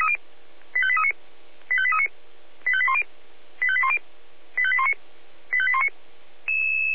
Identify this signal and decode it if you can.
Five-tone selective call (ZVEI), heard as quick runs of five tones, tones of about 70 ms. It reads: 85839 85839 85810 85810 85819 85819